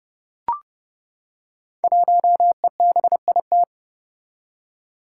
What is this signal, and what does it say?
Morse code, heard as a keyed tone, about 30 wpm, 700 Hz. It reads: 1EBIT
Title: Five Challenging Letter-Number Combinations 30wpm